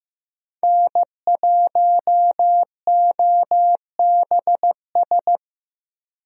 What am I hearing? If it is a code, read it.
Morse code, heard as a keyed tone, about 15 wpm, 700 Hz. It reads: N1OBS
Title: USA Call Signs - 5 Characters - Repeated 3x - 15wpm